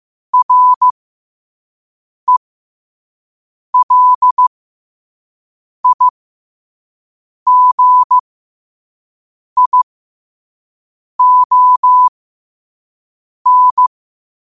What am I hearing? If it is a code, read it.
Morse code, heard as a keyed tone, about 15 wpm, spaced as for 5 wpm, 1000 Hz. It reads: RELIGION
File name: qotdfiles / 5wpm